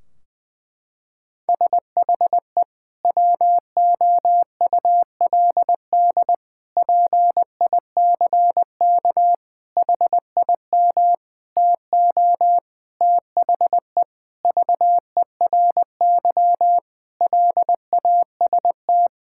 Morse code, heard as a keyed tone, about 21 wpm, 700 Hz. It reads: SHE WOULD PICK HIM TO THE VERY LAST
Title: Sentences from Top 1600 Words 20wpm